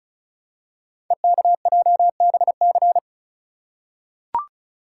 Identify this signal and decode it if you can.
Morse code, heard as a keyed tone, about 36 wpm, 690 Hz. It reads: EKJBC
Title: Code Groups - 5 Letters - 35wpm